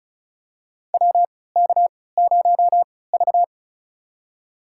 Morse code, heard as a keyed tone, about 36 wpm, spaced as for 20 wpm, 700 Hz. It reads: WK0V
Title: USA Call Signs - 4 Characters - 3x Character Spacing - 35wpm